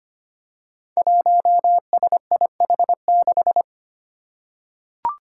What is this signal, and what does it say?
Morse code, heard as a keyed tone, about 25 wpm, 700 Hz. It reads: 1SIH6
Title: Five Challenging Letter-Number Combinations 25wpm